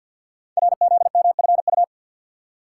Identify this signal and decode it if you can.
Morse code, heard as a keyed tone, about 53 wpm, 700 Hz. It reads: RZMFU
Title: Code Groups - 5 Letter:Numbers - 50wpm